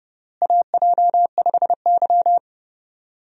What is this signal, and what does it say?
Morse code, heard as a keyed tone, about 30 wpm, 700 Hz. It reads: AJ5Y
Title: USA Call Signs - 4 Characters - Repeated 3x - 30wpm